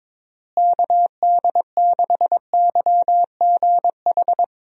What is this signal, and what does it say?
Morse code, heard as a keyed tone, about 22 wpm, 700 Hz. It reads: KD6YGH
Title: USA Call Signs - 6 Characters - Repeated 2x - 22wpm